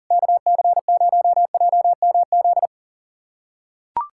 Morse code, heard as a keyed tone, about 43 wpm, 700 Hz. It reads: KC0JMZ
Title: SOTA QSO Practice 40wpm